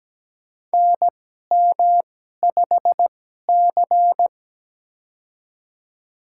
Morse code, heard as a keyed tone, about 17 wpm, spaced as for 12 wpm, 700 Hz. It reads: NM5C
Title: USA Call Signs - 2x Character Spacing - 17wpm